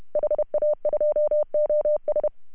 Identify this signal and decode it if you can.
Morse code, heard as a keyed tone, about 33 wpm, 600 Hz. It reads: HA2OS
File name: CW / HA2OS RHOB 2017 CW